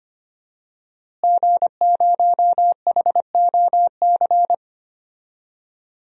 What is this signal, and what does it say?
Morse code, heard as a keyed tone, about 25 wpm, 700 Hz. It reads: G0HOC
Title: DXCC-100 Call Signs - 5 Characters - Repeated 3x - 25wpm